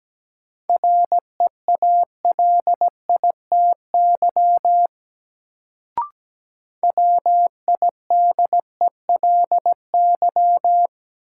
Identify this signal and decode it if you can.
Morse code, heard as a keyed tone, about 17 wpm, 700 Hz. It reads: REALITY WIDELY
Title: Sentences from Top 1200 Words - New Word Drill - 17wpm